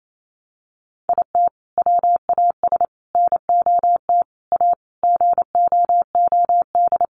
Morse code, heard as a keyed tone, about 28 wpm, 700 Hz. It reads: IT WAS NOT A GOOD
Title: Sentences from Top 1100 Words 28wpm